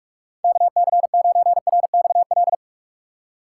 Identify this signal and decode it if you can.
Morse code, heard as a keyed tone, about 41 wpm, 690 Hz. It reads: KC9RXL
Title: USA Call Signs - 6 Characters - 2x Character Spacing - 45wpm